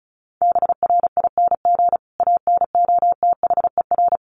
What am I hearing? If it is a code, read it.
Morse code, heard as a keyed tone, about 34 wpm, 700 Hz. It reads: BRING ANOTHER
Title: Sentences from Top 1300 Words 35wpm